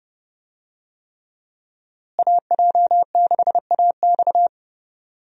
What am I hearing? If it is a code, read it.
Morse code, heard as a keyed tone, about 32 wpm, 700 Hz. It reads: AJ6AX
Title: USA Call Signs - 5 Characters - Repeated 3x - 30wpm